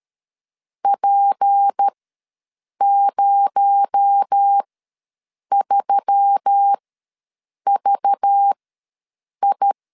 Morse code, heard as a keyed tone, about 13 wpm, spaced as for 7 wpm, 800 Hz. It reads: P03VI
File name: LEZIONI / 8 YJ45 / 8 YJ ws 12wpm 3342